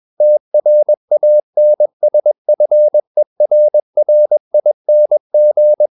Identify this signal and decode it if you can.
Morse code, heard as a keyed tone, about 21 wpm, 600 Hz. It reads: TRANSFERRING
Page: CW med Gnister